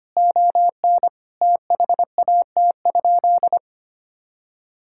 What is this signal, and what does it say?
Morse code, heard as a keyed tone, about 25 wpm, 700 Hz. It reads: ON THAT?
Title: Sentences from Top 1600 Words 25wpm